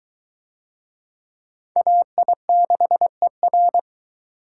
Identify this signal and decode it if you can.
Morse code, heard as a keyed tone, about 22 wpm, 700 Hz. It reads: AI6ER